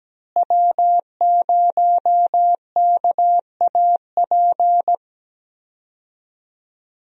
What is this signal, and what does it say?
Morse code, heard as a keyed tone, about 17 wpm, 700 Hz. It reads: W0KAP
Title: USA Call Signs - 5 Characters - Repeated 3x - 17wpm